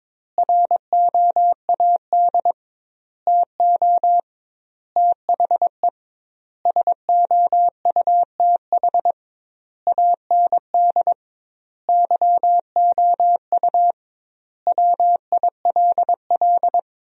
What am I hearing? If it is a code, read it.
Morse code, heard as a keyed tone, about 22 wpm, 700 Hz. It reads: ROAD TO THE SOUTH AND YOU WILL
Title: 2x Word-Spacing - Sentences from Top 400 Words - 22wpm